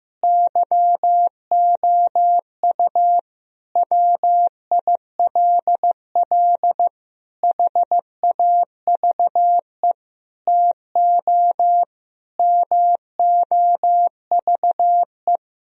Morse code, heard as a keyed tone, about 16 wpm, 700 Hz. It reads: YOU WILL HAVE TO MOVE
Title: Sentences from Top 1000 Words - Review Part 4 - 15wpm